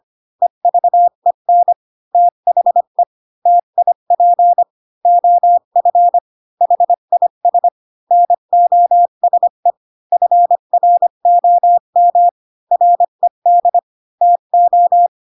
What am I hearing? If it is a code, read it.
Morse code, heard as a keyed tone, about 25 wpm, 700 Hz. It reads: EVEN THE TIP OF HIS NOSE FROM RED TO
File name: Pinocchio / Pinocchio ch1 20 25 ik1hge